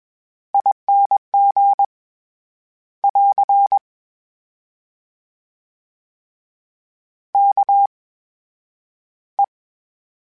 Morse code, heard as a keyed tone, about 21 wpm, spaced as for 10 wpm, 800 Hz. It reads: ING <AR> KE
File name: Special INGs-mp3s / Special INGs-mp3s / VerySlowToFaster Ing @20